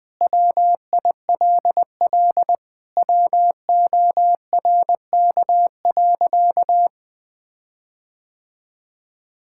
Morse code, heard as a keyed tone, about 21 wpm, 700 Hz. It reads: WILL WORK.
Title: Sentences with 6-Word Phrases - Spoken Beginning - 20wpm